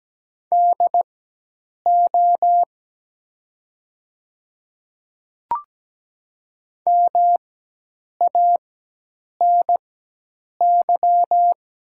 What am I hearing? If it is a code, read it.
Morse code, heard as a keyed tone, about 17 wpm, spaced as for 8 wpm, 690 Hz. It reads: DO MANY
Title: 4x Character Spacing - Top 200 Words - 17wpm